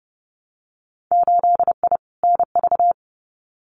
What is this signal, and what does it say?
Morse code, heard as a keyed tone, about 30 wpm, 700 Hz. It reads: 8I NV
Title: Winter Field Day Exchanges Encore 30wpm